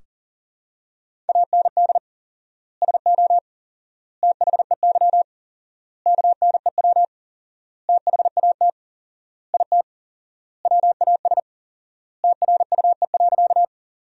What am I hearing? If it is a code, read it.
Morse code, heard as a keyed tone, about 37 wpm, 700 Hz. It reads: AND SO THEY KNEW THAT IT WAS TRUE.
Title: 4x Word-Spacing - Sentences from Top 400 Words - 40wpm